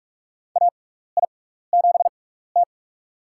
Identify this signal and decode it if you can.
Morse code, heard as a keyed tone, about 46 wpm, spaced as for 16 wpm, 700 Hz. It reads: AI7T
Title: USA Call Signs - 4 Characters - 6x Character Spacing - 45wpm